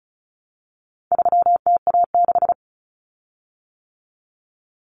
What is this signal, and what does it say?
Morse code, heard as a keyed tone, about 34 wpm, 700 Hz. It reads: 3TA6
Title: Common Cut Numbers - 4-Digit 35wpm